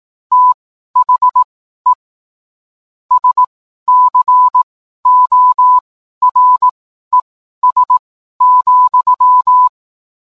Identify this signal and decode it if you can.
Morse code, heard as a keyed tone, about 18 wpm, spaced as for 12 wpm, 1000 Hz. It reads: THE SCORES,